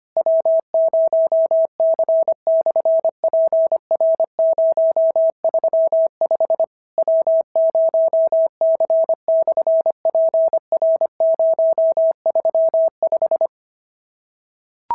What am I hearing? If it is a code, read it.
Morse code, heard as a keyed tone, about 25 wpm, 650 Hz. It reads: W0C/PR035 W0C/PR035
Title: SOTA QSO Practice 25wpm